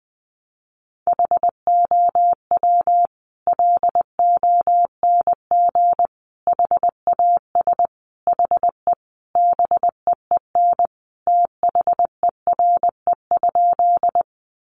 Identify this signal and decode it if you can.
Morse code, heard as a keyed tone, about 20 wpm, 700 Hz. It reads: HOW LONG HAS HE BEEN THERE?
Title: Sentences from Top 100 Words 20wpm